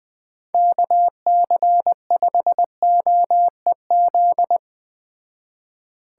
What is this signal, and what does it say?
Morse code, heard as a keyed tone, about 20 wpm, 700 Hz. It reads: KC5OEZ